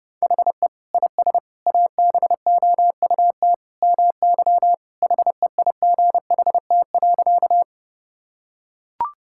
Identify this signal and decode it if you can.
Morse code, heard as a keyed tone, about 32 wpm, 700 Hz. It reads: HE IS ABOUT MY HEIGHT.
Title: Sentences from Top 1600 Words 30wpm